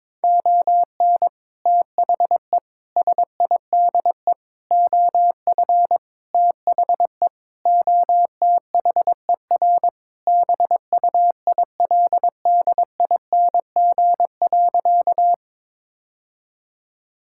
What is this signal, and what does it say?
Morse code, heard as a keyed tone, about 22 wpm, 700 Hz. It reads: ON THE SIDE OF THE OTHER BUILDING.
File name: practice / 22wpm / Sentences with 3-Word Phrases - 22wpm